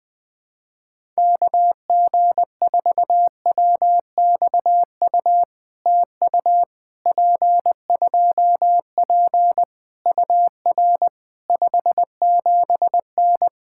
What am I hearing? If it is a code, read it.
Morse code, heard as a keyed tone, about 20 wpm, 700 Hz. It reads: KG4WXU TU P2P UR 57N
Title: POTA QSO Practice - 20wpm